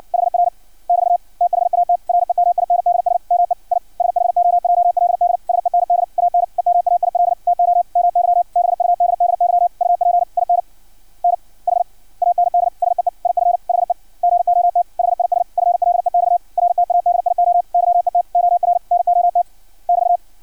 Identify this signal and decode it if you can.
Morse code, heard as a keyed tone, about 56 wpm, 700 Hz. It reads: HR <BT> THAT REMINDS ME I SHOJLD SEND AN EMAIL TO MY BUDDY UP IN NH AND SEE IF HE GOT HIS FLEX RTADIO YET OR NOT <BT>
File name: archived / CRSnet